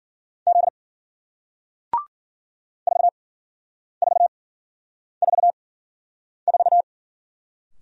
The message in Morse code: Z 4444

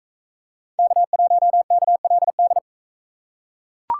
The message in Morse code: K1KLD

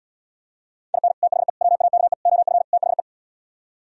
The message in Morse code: IN THE WAKE OF THE

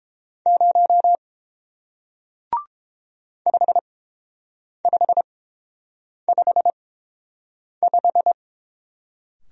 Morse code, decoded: 0 5555

